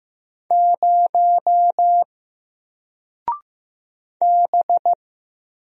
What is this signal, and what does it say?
Morse code, heard as a keyed tone, about 15 wpm, 700 Hz. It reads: 0 B
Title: Single Character (Letter:Number:Punctuation:Prosign) Alt 15wpm